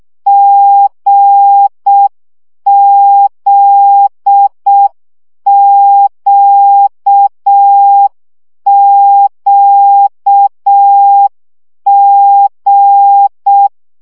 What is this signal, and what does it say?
Morse code, heard as a keyed tone, about 6 wpm, 800 Hz. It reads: GZQQG
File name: CW gyakorlas g z q 5 baud